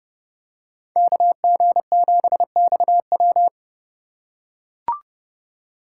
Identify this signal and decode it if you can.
Morse code, heard as a keyed tone, about 30 wpm, 700 Hz. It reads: KG7XW
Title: USA Call Signs - 5 Characters - 4x Character Spacing - 30wpm